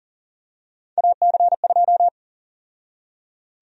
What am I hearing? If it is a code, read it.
Morse code, heard as a keyed tone, about 38 wpm, 700 Hz. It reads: AC2
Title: USA Call Sign Prefixes - 40wpm